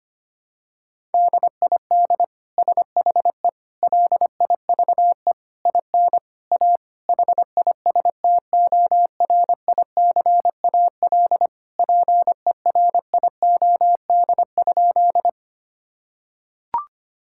Morse code, heard as a keyed tone, about 25 wpm, 700 Hz. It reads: DID SHE LIVE IN A HISTORICAL PERIOD?
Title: Sentences from Top 1000 Words - Review Part 5 - 25wpm